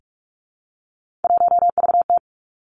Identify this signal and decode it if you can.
Morse code, heard as a keyed tone, about 46 wpm, 700 Hz. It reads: 1VT
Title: Three Challenging Letter-Number Combinations 45wpm